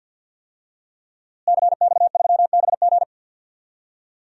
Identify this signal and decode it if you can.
Morse code, heard as a keyed tone, about 46 wpm, 700 Hz. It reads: CX3BG
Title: DXCC-100 Call Signs - 5 Characters - Repeated 2x - 50wpm